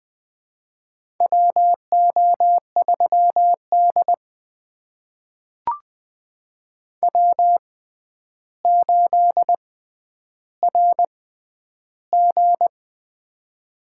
Morse code, heard as a keyed tone, about 20 wpm, spaced as for 10 wpm, 700 Hz. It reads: WO3D W8RG
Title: USA Call Signs - 4 Characters - 6x Character Spacing - 20wpm